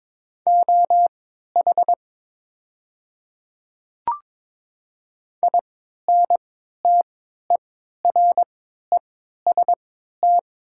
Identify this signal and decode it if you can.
Morse code, heard as a keyed tone, about 22 wpm, spaced as for 13 wpm, 690 Hz. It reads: OH INTEREST
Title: 3x Character Spacing - Top 500 Words - 22wpm